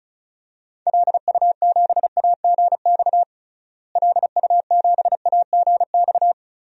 Morse code, heard as a keyed tone, about 34 wpm, 700 Hz. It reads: LU7AGX LU7AGX
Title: DXCC-100 Call Signs - 6 Characters - Repeated 3x - 35wpm